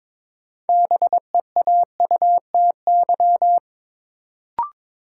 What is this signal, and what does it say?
Morse code, heard as a keyed tone, about 22 wpm, 700 Hz. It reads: BEAUTY
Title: Sentences from Top 600 Words - New Word Drill - 22wpm